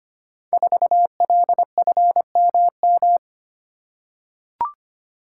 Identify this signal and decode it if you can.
Morse code, heard as a keyed tone, about 25 wpm, 700 Hz. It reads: 4LFMM
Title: Code Groups - 5 Letter:Numbers - 25wpm